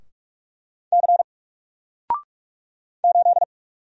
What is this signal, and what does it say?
Morse code, heard as a keyed tone, about 44 wpm, 700 Hz. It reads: C 8
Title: Single Letter-Number Alt 45wpm